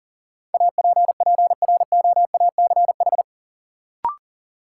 Morse code, heard as a keyed tone, about 38 wpm, 700 Hz. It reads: APPROACH
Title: Sentences from Top 1000 Words - Review Part 6 - New Word Drill - 40wpm